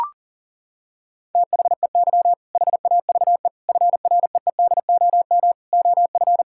Morse code, heard as a keyed tone, about 40 wpm, 700 Hz. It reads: THEY HAVE FREEDOM OF
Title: Sentences from Top 1000 Words - Review Part 5 - 40wpm